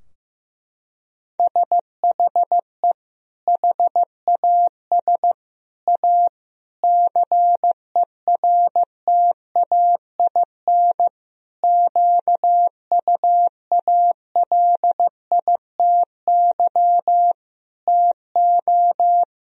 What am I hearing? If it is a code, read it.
Morse code, heard as a keyed tone, about 15 wpm, 700 Hz. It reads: SHE HAS A CERTAIN QUALITY TO
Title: Sentences from Top 1000 Words - Review Part 4 - 15wpm